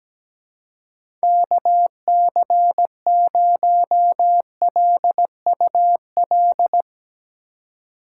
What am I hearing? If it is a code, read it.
Morse code, heard as a keyed tone, about 17 wpm, 700 Hz. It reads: KC0LUL
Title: USA Call Signs - 6 Characters - 6x Character Spacing - 17wpm